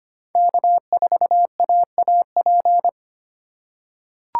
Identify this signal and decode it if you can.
Morse code, heard as a keyed tone, about 25 wpm, 700 Hz. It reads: K4AAP
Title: USA Call Signs - 5 Characters - 4x Character Spacing - 25wpm